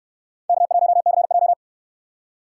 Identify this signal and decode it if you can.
Morse code, heard as a keyed tone, about 68 wpm, 700 Hz. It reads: K1CJ